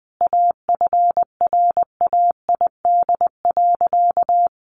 Morse code, heard as a keyed tone, about 20 wpm, 700 Hz. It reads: AFRAID.
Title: Sentences from Top 1500 Words 20wpm